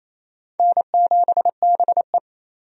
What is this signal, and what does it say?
Morse code, heard as a keyed tone, about 28 wpm, 700 Hz. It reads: N7BE
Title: USA Call Signs - 4 Characters - 28wpm